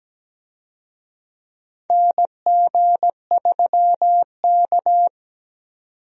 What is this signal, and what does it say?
Morse code, heard as a keyed tone, about 17 wpm, 700 Hz. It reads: NG3K
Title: USA Call Signs - 4 Characters - 17wpm